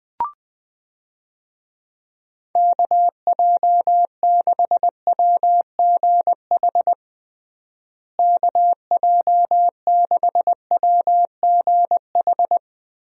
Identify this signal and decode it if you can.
Morse code, heard as a keyed tone, about 20 wpm, 700 Hz. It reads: KJ6WGH KJ6WGH
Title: USA Call Signs - 6 Characters - Repeated 2x - 20wpm